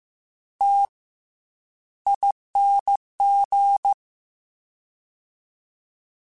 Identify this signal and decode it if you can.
Morse code, heard as a keyed tone, about 15 wpm, 800 Hz. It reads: T ING
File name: Special INGs / Special INGs-mp3s / SlowToFaster ing @13